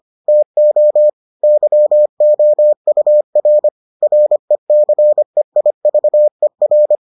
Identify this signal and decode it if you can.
Morse code, heard as a keyed tone, about 25 wpm, 600 Hz. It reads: TO YOUR RECEIVER